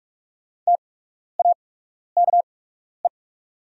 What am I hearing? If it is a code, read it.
Morse code, heard as a keyed tone, about 45 wpm, spaced as for 12 wpm, 690 Hz. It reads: TAKE
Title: 8x Character Spacing - Top 500 Words - 45wpm